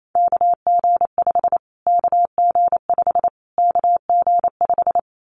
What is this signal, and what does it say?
Morse code, heard as a keyed tone, about 28 wpm, 700 Hz. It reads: KG5 KG5 KG5
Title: Callsign Prefixes for Top 25 DXCC - Repeated 3x - 28wpm